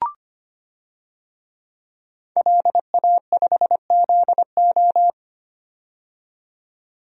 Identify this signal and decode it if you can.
Morse code, heard as a keyed tone, about 25 wpm, 700 Hz. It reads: LA5ZO
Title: DXCC-100 Call Signs - 5 Characters - 25wpm